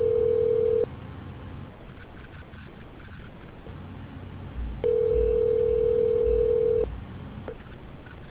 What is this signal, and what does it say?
Ringback tone (North America), 440 + 480 Hz.